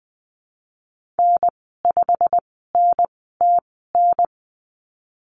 Morse code, heard as a keyed tone, about 20 wpm, spaced as for 14 wpm, 700 Hz. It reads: N5NTN
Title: 088 Lesson 15g - Number 5 - Call Signs 20wpm